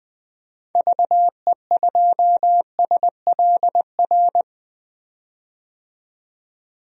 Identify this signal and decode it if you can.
Morse code, heard as a keyed tone, about 21 wpm, 700 Hz. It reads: VE2SLR
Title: DXCC-100 Call Signs - 6 Characters - 20wpm